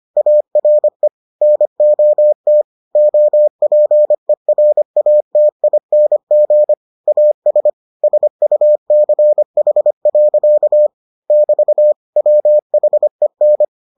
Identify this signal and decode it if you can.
Morse code, heard as a keyed tone, about 25 wpm, 600 Hz. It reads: ARE NOT OPERATING AS SUCH. <BT> WHEN